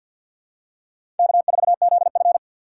Morse code, heard as a keyed tone, about 53 wpm, 700 Hz. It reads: K4ZF
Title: USA Call Signs - 4 Characters - 6x Character Spacing - 50wpm